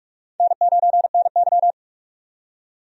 Morse code, heard as a keyed tone, about 42 wpm, 700 Hz. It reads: N9NY
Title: USA Call Signs - 4 Characters - 45wpm